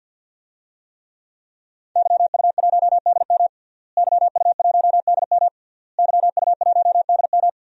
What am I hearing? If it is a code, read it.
Morse code, heard as a keyed tone, about 47 wpm, 700 Hz. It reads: YU1DM YU1DM YU1DM
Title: DXCC-100 Call Signs - 5 Characters - Repeated 3x - 50wpm